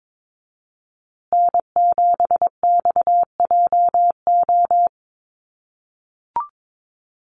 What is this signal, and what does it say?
Morse code, heard as a keyed tone, about 22 wpm, 700 Hz. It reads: N7XJO
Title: USA Call Signs Repeated 2x 22wpm